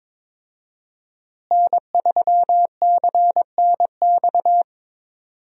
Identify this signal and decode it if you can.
Morse code, heard as a keyed tone, about 22 wpm, 700 Hz. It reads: N3CNX
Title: USA Call Signs - 5 Characters - Repeated 2x - 22wpm